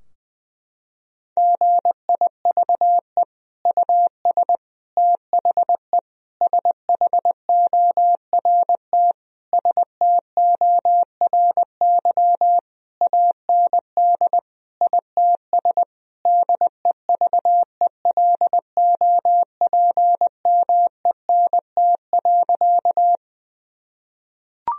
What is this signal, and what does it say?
Morse code, heard as a keyed tone, about 20 wpm, 700 Hz. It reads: GIVE US THE SHORT STORY AND ITS DEVELOPMENT.
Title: Sentences from Top 1000 Words - Review Part 4 - 20wpm